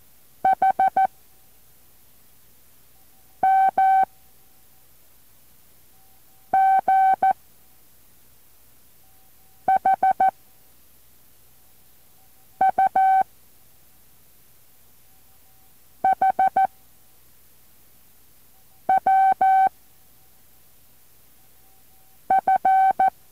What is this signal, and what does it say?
Morse code, heard as a keyed tone, about 14 wpm, 770 Hz. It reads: H M G H U H W F